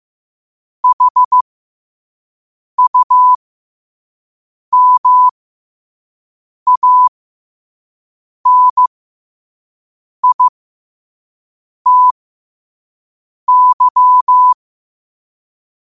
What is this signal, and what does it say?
Morse code, heard as a keyed tone, about 15 wpm, spaced as for 5 wpm, 1000 Hz. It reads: HUMANITY
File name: qotdfiles / 5wpm